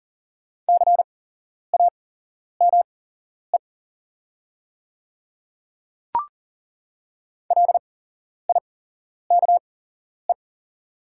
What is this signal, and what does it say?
Morse code, heard as a keyed tone, about 38 wpm, spaced as for 11 wpm, 700 Hz. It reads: CAME LIKE
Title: 8x Character Spacing - Top 300 Words - 40wpm